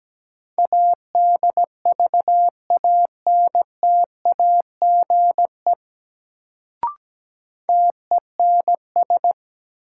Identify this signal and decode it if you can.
Morse code, heard as a keyed tone, about 17 wpm, 700 Hz. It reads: ADVANTAGE TENS
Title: Sets of 4 Words - New Word Drill - 17wpm